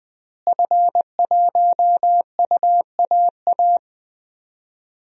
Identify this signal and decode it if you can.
Morse code, heard as a keyed tone, about 21 wpm, 700 Hz. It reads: F1UAA
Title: DXCC-100 Call Signs - 5 Characters - Repeated 3x - 20wpm